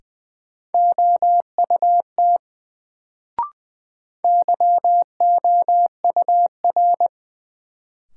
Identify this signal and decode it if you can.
Morse code, heard as a keyed tone, about 20 wpm, 700 Hz. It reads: OUT YOUR
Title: Sets of 2 Words - New Word Drill - 20wpm